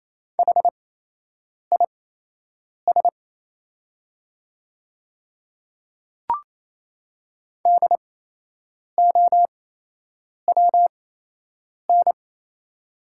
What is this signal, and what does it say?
Morse code, heard as a keyed tone, about 28 wpm, spaced as for 8 wpm, 700 Hz. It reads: HIS DOWN
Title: 8x Character Spacing - Top 100 Words - 28wpm